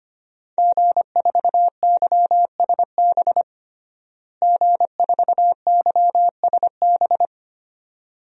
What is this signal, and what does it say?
Morse code, heard as a keyed tone, about 25 wpm, 700 Hz. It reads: G4YSB G4YSB
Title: DXCC-100 Call Signs - 5 Characters - Repeated 2x - 25wpm